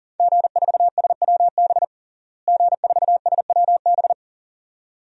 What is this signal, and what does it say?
Morse code, heard as a keyed tone, about 40 wpm, 700 Hz. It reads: G4SWB G4SWB
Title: DXCC-100 Call Signs - 5 Characters - Repeated 2x - 40wpm